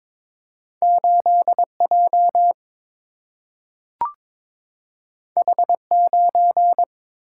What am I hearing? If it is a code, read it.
Morse code, heard as a keyed tone, about 22 wpm, 700 Hz. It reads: 8 J H 9
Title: Single-Letter Single-Number 22wpm